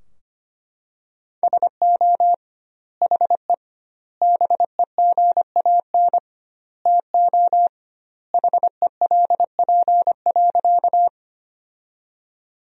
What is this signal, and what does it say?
Morse code, heard as a keyed tone, about 25 wpm, 700 Hz. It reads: SO HE BEGAN TO HELP.
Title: 2x Word-Spacing - Sentences from Top 300 Words - 25wpm